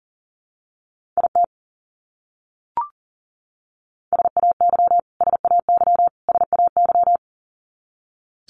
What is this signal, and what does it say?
Morse code, heard as a keyed tone, about 40 wpm, 700 Hz. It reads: IT SAY SAY SAY
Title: Top 300 Words Repeated 40wpm